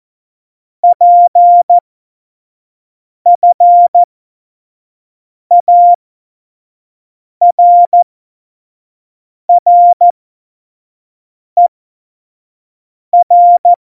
Morse code, heard as a keyed tone, about 14 wpm, spaced as for 5 wpm, 700 Hz. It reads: PFARRER
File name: Morsekurs / Witze / Witz 01. Dezember, 2020 25 F70